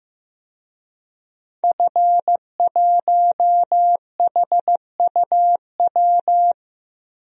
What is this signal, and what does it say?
Morse code, heard as a keyed tone, about 15 wpm, 700 Hz. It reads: F1HUW
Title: DXCC-100 Call Signs - 5 Characters - Repeated 3x - 15wpm